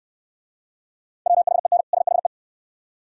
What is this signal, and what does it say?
Morse code, heard as a keyed tone, about 69 wpm, 700 Hz. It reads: WHEN SHE